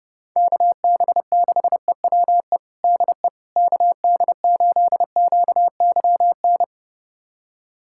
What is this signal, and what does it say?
Morse code, heard as a keyed tone, about 30 wpm, 700 Hz. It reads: KB6EWE DE KD8QYN
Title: USA Callsign DE Callsign - 6 Characters - 30wpm